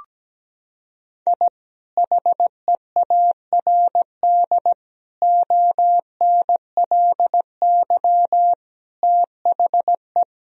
Morse code, heard as a keyed tone, about 17 wpm, 700 Hz. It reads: I HEARD ONLY THE
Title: Sentences from Top 1000 Words - Review Part 2 - 17wpm